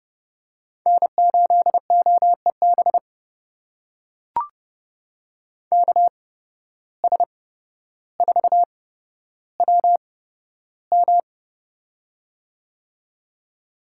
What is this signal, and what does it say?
Morse code, heard as a keyed tone, about 32 wpm, spaced as for 13 wpm, 700 Hz. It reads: N8OEB KS4WM
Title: USA Call Signs - 8x Character Spacing - 30wpm